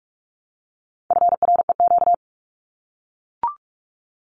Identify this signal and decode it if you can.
Morse code, heard as a keyed tone, about 44 wpm, 700 Hz. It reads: FREQ
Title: Abbreviations Repeated 3x - 45wpm